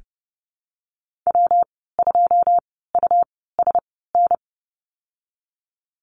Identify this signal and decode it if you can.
Morse code, heard as a keyed tone, about 32 wpm, spaced as for 17 wpm, 700 Hz. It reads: W2USN
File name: Lesson 19g - Letter w - Call Signs 30wpm